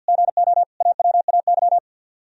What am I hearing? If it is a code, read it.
Morse code, heard as a keyed tone, about 64 wpm, 700 Hz. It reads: GO AWAY